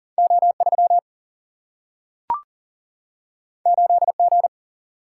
Morse code, heard as a keyed tone, about 42 wpm, 700 Hz. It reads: O 3 8 G
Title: Single-Letter Single-Number 40wpm